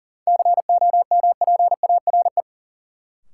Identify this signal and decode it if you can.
Morse code, heard as a keyed tone, about 40 wpm, 700 Hz. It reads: COMPARE
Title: Sentences from Top 1300 Words - New Word Drill - 40wpm